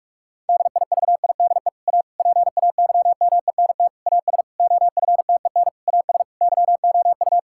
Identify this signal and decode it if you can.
Morse code, heard as a keyed tone, about 41 wpm, 700 Hz. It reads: DIVIDE A PAYMENT AS OFTEN AS YOU
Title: Sentences from Top 1600 Words 45wpm